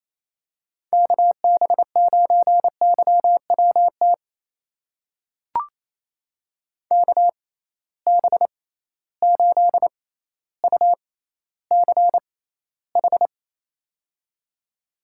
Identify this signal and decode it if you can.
Morse code, heard as a keyed tone, about 28 wpm, spaced as for 14 wpm, 700 Hz. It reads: KB9YWT KB8UCH
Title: USA Call Signs - 6 Characters - 6x Character Spacing - 28wpm